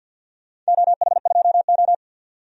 Morse code, heard as a keyed tone, about 50 wpm, 700 Hz. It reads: OH2O
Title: DXCC-100 Call Signs - 4 Characters - 50wpm